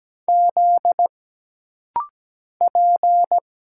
Morse code, heard as a keyed tone, about 17 wpm, 700 Hz. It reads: Z P
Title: Single Letters - Rapid-Fire 17wpm